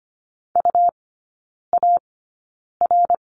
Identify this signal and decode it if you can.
Morse code, heard as a keyed tone, about 24 wpm, 700 Hz. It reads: U A R
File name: course / 25:10wpm / 100 Lesson 17e - Letter u - Three Characters 25wpm